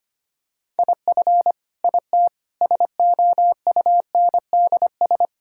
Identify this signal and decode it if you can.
Morse code, heard as a keyed tone, about 25 wpm, 700 Hz. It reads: IF IT SOUNDS
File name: practice / 25wpm / Sentences with 9-Word Phrases - 25wpm